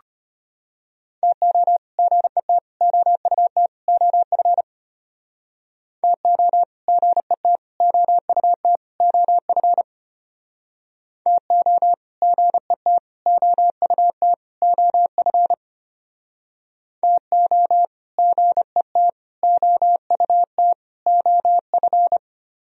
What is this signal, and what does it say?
Morse code, heard as a keyed tone, about 31 wpm, 700 Hz. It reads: TO GET OUT OF TO GET OUT OF TO GET OUT OF TO GET OUT OF